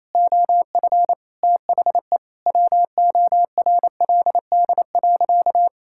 Morse code, heard as a keyed tone, about 28 wpm, 700 Hz. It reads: OF THE WORLD.
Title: Sentences from Top 1000 Words - Review Part 5 - 28wpm